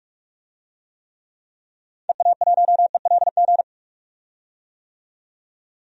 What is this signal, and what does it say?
Morse code, heard as a keyed tone, about 46 wpm, 700 Hz. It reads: EA1ELG
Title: DXCC-100 Call Signs - 6 Characters - 45wpm